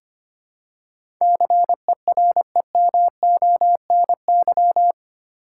Morse code, heard as a keyed tone, about 25 wpm, 700 Hz. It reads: CEREMONY
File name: 8 Letter Words - First 3 Letters Given - 25wpm